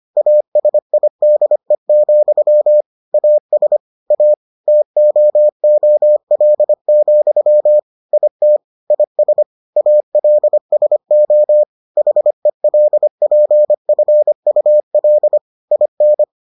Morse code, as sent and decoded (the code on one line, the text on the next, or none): .- ... .. -.. --..-- / .- ... / .- / - --- --- .-.. --..-- / .. - / .. ... / .- .-.. ... --- / .... .-.. .--. ..-. ..- .-.. / .. -.
ASIDE, AS A TOOL, IT IS ALSO HELPFUL IN